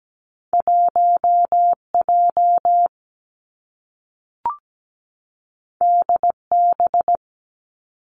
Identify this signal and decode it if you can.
Morse code, heard as a keyed tone, about 17 wpm, 700 Hz. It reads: 1 J D B
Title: Two Challenging Letter-Number Combinations 17wpm